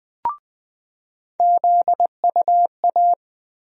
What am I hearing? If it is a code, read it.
Morse code, heard as a keyed tone, about 21 wpm, 700 Hz. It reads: ZUA
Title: USA Call Sign Postfixes - 3 Letters 20wpm